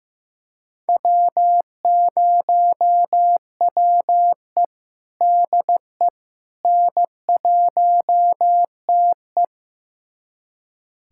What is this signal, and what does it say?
Morse code, heard as a keyed tone, about 15 wpm, 700 Hz. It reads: W0WE DE N1TE
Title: USA Callsign DE Callsign - 4 Characters - 15wpm